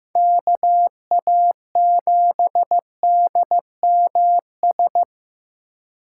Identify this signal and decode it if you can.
Morse code, heard as a keyed tone, about 15 wpm, 700 Hz. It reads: KA7DMS
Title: USA Call Signs - 6 Characters - 2x Character Spacing - 15wpm